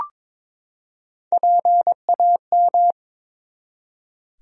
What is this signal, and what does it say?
Morse code, heard as a keyed tone, about 21 wpm, 700 Hz. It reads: PAM
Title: USA Common Names - Max 3 Letters - 22wpm